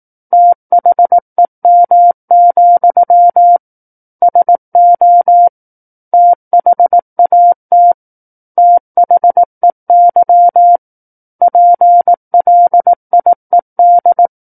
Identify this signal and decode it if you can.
Morse code, heard as a keyed tone, about 18 wpm, 700 Hz. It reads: THEM, SO THAT THEY PLIED
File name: Book One / 09-Chapter 9 - 18 WPM